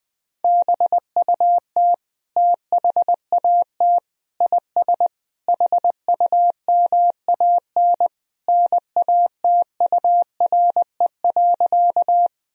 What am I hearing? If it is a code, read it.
Morse code, heard as a keyed tone, about 20 wpm, 700 Hz. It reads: BUT THAT IS HUMAN NATURE.
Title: Sentences from Top 1000 Words - Review Part 3 - 20wpm